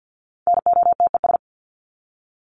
Morse code, heard as a keyed tone, about 50 wpm, 700 Hz. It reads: NOTES
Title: ES:S Suffix - Top 100 Words with Base Words 50wpm